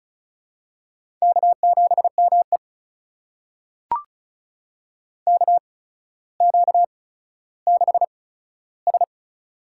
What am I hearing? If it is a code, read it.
Morse code, heard as a keyed tone, about 36 wpm, spaced as for 15 wpm, 700 Hz. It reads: K7ME KQ6S
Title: USA Call Signs - 4 Characters - 8x Character Spacing - 35wpm